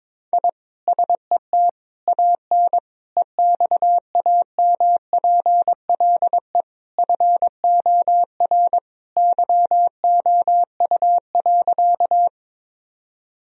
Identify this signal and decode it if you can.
Morse code, heard as a keyed tone, about 22 wpm, 700 Hz. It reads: I SET AN EXAMPLE FOR YOU.
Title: Sentences from Top 1000 Words - Review Part 1 - 22wpm